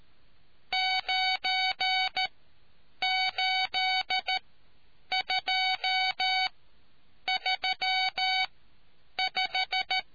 Morse code, heard as a keyed tone, about 13 wpm, spaced as for 8 wpm, 750 Hz. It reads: 98235